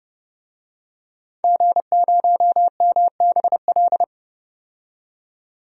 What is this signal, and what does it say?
Morse code, heard as a keyed tone, about 30 wpm, 700 Hz. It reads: G0MBL
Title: DXCC-100 Call Signs - 5 Characters - 30wpm